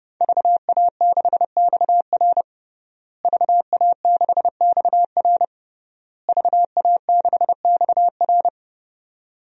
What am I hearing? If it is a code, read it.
Morse code, heard as a keyed tone, about 30 wpm, 700 Hz. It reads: VA6XR VA6XR VA6XR
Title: DXCC-100 Call Signs - 5 Characters - Repeated 3x - 30wpm